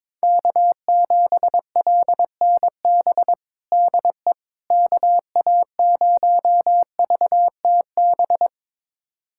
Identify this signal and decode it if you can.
Morse code, heard as a keyed tone, about 22 wpm, 700 Hz. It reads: K7LNB DE KA0VTB
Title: USA Callsign DE Callsign 22wpm